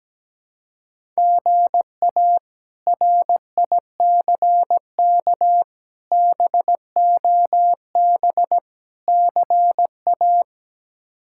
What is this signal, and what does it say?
Morse code, heard as a keyed tone, about 17 wpm, 700 Hz. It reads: GA RICK BOB CA
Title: K1USN SST QSO Practice - 17wpm